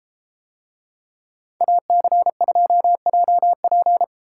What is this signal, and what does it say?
Morse code, heard as a keyed tone, about 31 wpm, 700 Hz. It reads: AC2JP